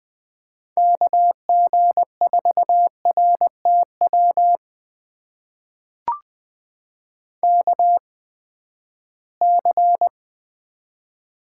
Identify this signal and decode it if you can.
Morse code, heard as a keyed tone, about 20 wpm, spaced as for 14 wpm, 700 Hz. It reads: KG4RTW KC
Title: USA Call Signs - 6 Characters - 8x Character Spacing - 20wpm